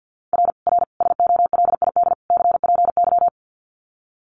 Morse code, heard as a keyed tone, about 50 wpm, 700 Hz. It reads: R R SOLID CPY
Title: Common QSO Phrases 50wpm